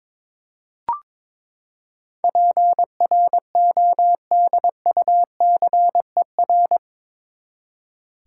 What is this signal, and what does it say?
Morse code, heard as a keyed tone, about 21 wpm, 700 Hz. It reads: PRODUCER
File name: practice / 22wpm / Eight Letter Words - 22wpm